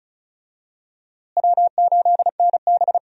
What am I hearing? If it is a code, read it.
Morse code, heard as a keyed tone, about 34 wpm, 700 Hz. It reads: W8NB
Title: USA Call Signs - 4 Characters - Repeated 2x - 35wpm